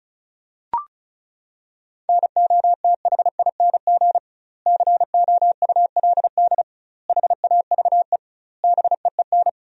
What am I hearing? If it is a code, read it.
Morse code, heard as a keyed tone, about 34 wpm, 700 Hz. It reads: NOTHING COULD HAVE BEEN
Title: 2x Word-Spacing - Sentences from Top 500 Words - 35wpm